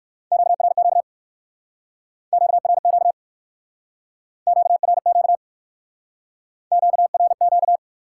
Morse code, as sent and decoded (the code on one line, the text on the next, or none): --.- .-. --.- / --.- .-. --.- / --.- .-. --.- / --.- .-. --.-
QRQ QRQ QRQ QRQ